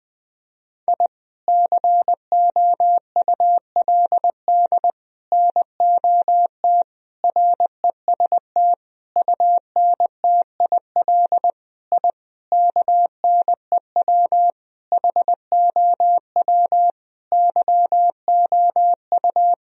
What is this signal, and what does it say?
Morse code, heard as a keyed tone, about 20 wpm, 700 Hz. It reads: I COULD NOT REST UNTIL I KNEW HOW YOU
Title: Sentences from Top 1000 Words - Review Part 1 - 20wpm